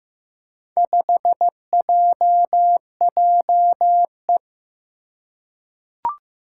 Morse code, heard as a keyed tone, about 15 wpm, 700 Hz. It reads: 5JJE
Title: Four Challenging Letter-Number Combinations 15wpm